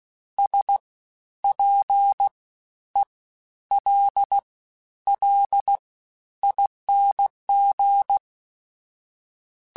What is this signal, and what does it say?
Morse code, heard as a keyed tone, about 16 wpm, 800 Hz. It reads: S P E L L ING
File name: Special INGs-mp3s / Special INGs-mp3s / Fast ing @15